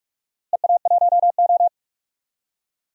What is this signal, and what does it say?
Morse code, heard as a keyed tone, about 46 wpm, 700 Hz. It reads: EA1O
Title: DXCC-100 Call Signs - 4 Characters - Repeated 2x - 45wpm